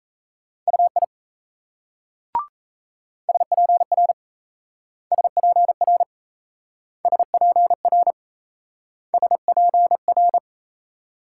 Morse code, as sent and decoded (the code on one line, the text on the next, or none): ..- .. / ... .--. .-. / ... .--. .-. / ... .--. .-. / ... .--. .-.
UI SPR SPR SPR SPR